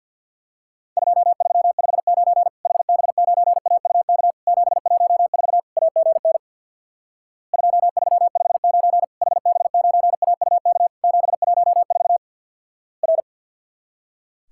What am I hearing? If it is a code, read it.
Morse code, heard as a keyed tone, about 50 wpm, 700 Hz. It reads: 2359 HB9AUK 714 AGN 2359 HB9AUK 714 R
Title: Worked All Europe DX Contest - QTC Exchange with Repeats - 50wpm